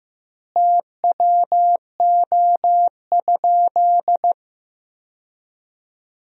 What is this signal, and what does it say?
Morse code, heard as a keyed tone, about 15 wpm, 700 Hz. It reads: TWO?
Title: 2x Word-Spacing - Sentences from Top 100 Words - 15wpm